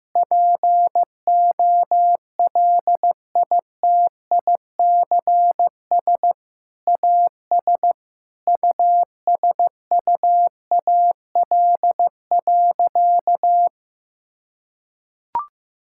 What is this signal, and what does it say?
Morse code, heard as a keyed tone, about 16 wpm, 700 Hz. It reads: POLITICS AS USUAL.
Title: Sentences from Top 1000 Words - Review Part 6 - 15wpm